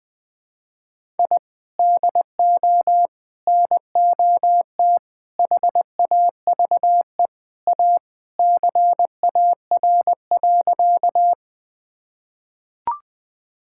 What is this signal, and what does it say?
Morse code, heard as a keyed tone, about 20 wpm, 700 Hz. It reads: I DO NOT HAVE A CAR.
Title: Sentences with 5-Word Phrases - 20wpm